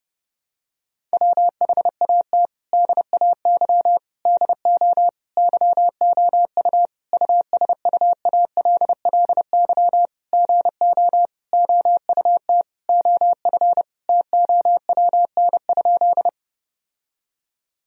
Morse code, heard as a keyed tone, about 30 wpm, 700 Hz. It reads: WHAT DAY DO YOU USUALLY GO OUT OF TOWN?
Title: Sentences from Top 1000 Words - Review Part 3 - 30wpm